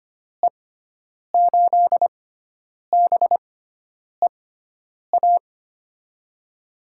Morse code, heard as a keyed tone, about 25 wpm, spaced as for 9 wpm, 700 Hz. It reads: E8BEA
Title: USA Call Signs - 6 Characters - 6x Character Spacing - 25wpm